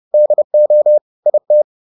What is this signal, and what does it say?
Morse code, heard as a keyed tone, about 30 wpm, 600 Hz. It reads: DO IT